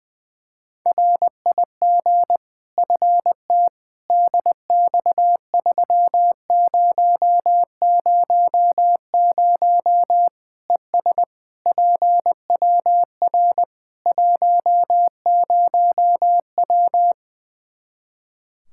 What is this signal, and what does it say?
Morse code, heard as a keyed tone, about 20 wpm, 700 Hz. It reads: RIG FT DX3000 ES PWR 10W
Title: Practice QSO - Exchange 2 - Info - 20wpm